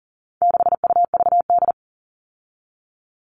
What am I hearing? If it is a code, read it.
Morse code, heard as a keyed tone, about 40 wpm, 700 Hz. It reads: 6UVD
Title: All Cut Numbers - 4-Digit 40wpm